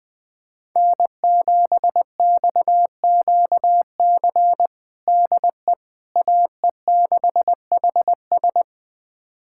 Morse code, heard as a keyed tone, about 20 wpm, 700 Hz. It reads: N7XQC DE AE6HS
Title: USA Callsign DE Callsign - 5 Characters - 20wpm